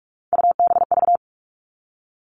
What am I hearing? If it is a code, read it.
Morse code, heard as a keyed tone, about 44 wpm, 700 Hz. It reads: UBV
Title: All Cut Numbers - 3-Digit 45wpm